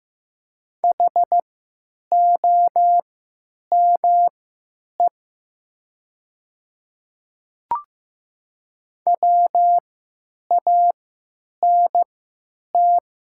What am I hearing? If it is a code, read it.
Morse code, heard as a keyed tone, about 15 wpm, spaced as for 9 wpm, 700 Hz. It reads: HOME WANT
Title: 3x Character Spacing - Top 200 Words - 15wpm